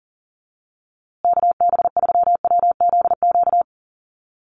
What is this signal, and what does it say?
Morse code, heard as a keyed tone, about 40 wpm, 700 Hz. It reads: KB3WZQ
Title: USA Call Signs Repeated 3x 40wpm